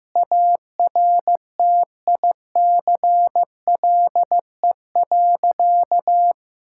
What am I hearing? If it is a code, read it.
Morse code, heard as a keyed tone, about 15 wpm, 700 Hz. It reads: ARTICLE.
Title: Sentences from Top 1000 Words - Review Part 5 - 15wpm